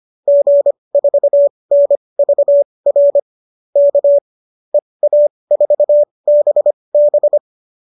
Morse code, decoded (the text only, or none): G4NVR K EA4BB